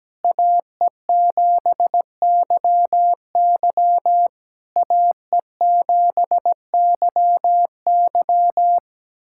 Morse code, heard as a keyed tone, about 18 wpm, 700 Hz. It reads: AE7YY AE7YY
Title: SOTA QSO Practice 17wpm